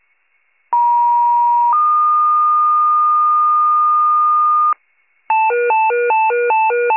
Two-tone page, heard as a one-second tone, then a three-second tone, about 950 then 1220 Hz, with noise. It is heard from 0.7 to 4.7 s.